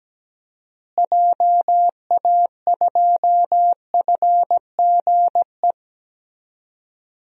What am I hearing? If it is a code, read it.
Morse code, heard as a keyed tone, about 17 wpm, 700 Hz. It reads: JA2FGE
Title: DXCC-100 Call Signs - 6 Characters - Repeated 2x - 17wpm